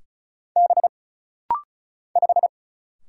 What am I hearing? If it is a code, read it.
Morse code, heard as a keyed tone, about 36 wpm, 700 Hz. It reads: B 5
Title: Single Letter-Number - Rapid-Fire 35wpm